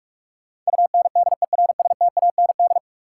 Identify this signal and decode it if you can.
Morse code, heard as a keyed tone, about 46 wpm, 700 Hz. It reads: UNDERSTAND
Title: Sets of 4 Words - New Word Drill - 45wpm